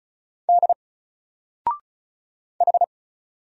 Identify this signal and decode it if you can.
Morse code, heard as a keyed tone, about 36 wpm, 700 Hz. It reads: D H
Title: Single Character (Letter:Number:Punctuation:Prosign) Alt 35wpm